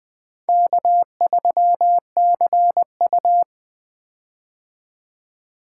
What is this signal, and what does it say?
Morse code, heard as a keyed tone, about 20 wpm, 700 Hz. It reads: K3CU
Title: 2023 POTA Hunter Call Signs - 20wpm